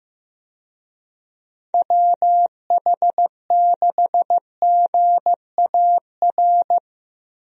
Morse code, heard as a keyed tone, about 16 wpm, 700 Hz. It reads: WH6GAR
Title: USA Call Signs - 6 Characters - Repeated 3x - 15wpm